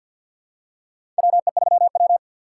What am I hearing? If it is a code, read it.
Morse code, heard as a keyed tone, about 52 wpm, 700 Hz. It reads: WE3W
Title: USA Call Signs - 4 Characters - 6x Character Spacing - 50wpm